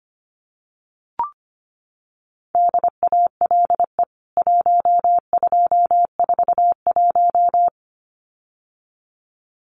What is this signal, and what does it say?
Morse code, heard as a keyed tone, about 25 wpm, 700 Hz. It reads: DALE 1241
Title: CWT Exchange 25wpm